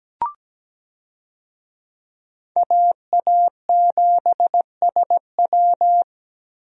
Morse code, heard as a keyed tone, about 17 wpm, 700 Hz. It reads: AA7SW
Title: USA Call Signs - 5 Characters - 17wpm